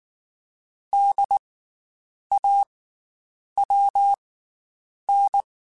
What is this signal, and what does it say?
Morse code, heard as a keyed tone, about 19 wpm, spaced as for 8 wpm, 800 Hz. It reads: DAWN
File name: Special INGs / Special INGs-mp3s / Slow ing @18